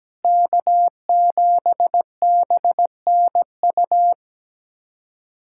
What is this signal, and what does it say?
Morse code, heard as a keyed tone, about 17 wpm, 700 Hz. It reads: K7BNU
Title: USA Callsign DE Callsign - 5 Characters - 17wpm